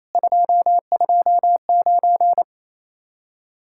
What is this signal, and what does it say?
Morse code, heard as a keyed tone, about 33 wpm, 700 Hz. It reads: 229
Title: Speed-Racing Common QSO Phrases Encore - 28wpm